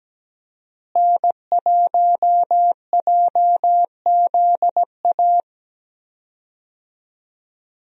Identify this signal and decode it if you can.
Morse code, heard as a keyed tone, about 17 wpm, 700 Hz. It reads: N1JZA
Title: USA Call Signs - 5 Characters - Repeated 2x - 17wpm